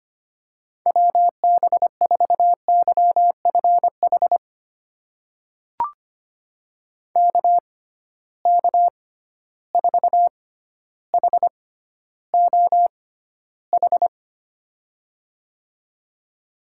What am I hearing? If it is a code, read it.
Morse code, heard as a keyed tone, about 25 wpm, spaced as for 13 wpm, 700 Hz. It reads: WB4YFH KK4HOH
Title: USA Call Signs - 6 Characters - 6x Character Spacing - 25wpm